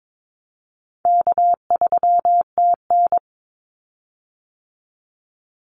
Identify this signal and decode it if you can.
Morse code, heard as a keyed tone, about 22 wpm, 700 Hz. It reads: K3TN
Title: Call Signs - 22wpm